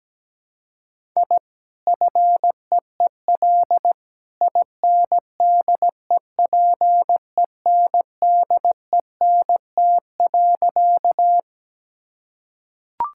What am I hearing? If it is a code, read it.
Morse code, heard as a keyed tone, about 17 wpm, 700 Hz. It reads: I FEEL INDEPENDENT.
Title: Sentences from Top 1000 Words - Review Part 6 - 17wpm